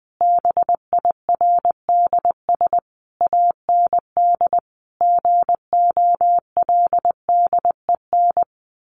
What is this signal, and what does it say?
Morse code, heard as a keyed tone, about 20 wpm, 700 Hz. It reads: BIRDS AND GOLDEN
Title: A Sound Of Thunder 20wpm